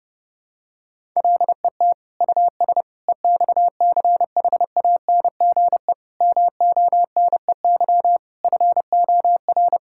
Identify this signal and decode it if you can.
Morse code, heard as a keyed tone, about 32 wpm, 700 Hz. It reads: LET US EXCHANGE MONEY FOR
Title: Sentences from Top 1000 Words - Review Part 6 - 30wpm